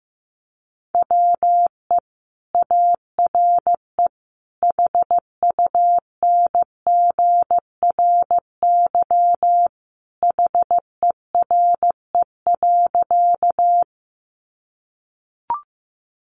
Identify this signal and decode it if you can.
Morse code, heard as a keyed tone, about 15 wpm, 700 Hz. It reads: WE ARE HUNGRY HERE.
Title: Sentences from Top 1400 Words 15wpm